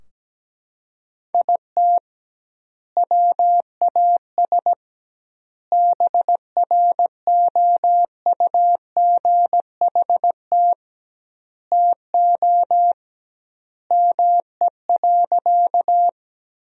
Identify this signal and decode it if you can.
Morse code, heard as a keyed tone, about 17 wpm, 700 Hz. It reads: IT WAS BROUGHT TO ME.
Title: 2x Word-Spacing - Sentences from Top 500 Words - 17wpm